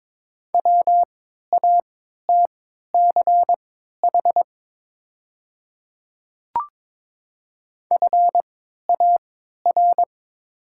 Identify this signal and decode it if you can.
Morse code, heard as a keyed tone, about 22 wpm, spaced as for 12 wpm, 700 Hz. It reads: WATCH FAR
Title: 3x Character Spacing - Top 300 Words - 22wpm